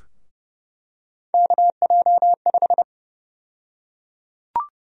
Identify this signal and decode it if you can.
Morse code, heard as a keyed tone, about 30 wpm, 700 Hz. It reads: KJ5
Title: USA Call Sign Prefixes - 30wpm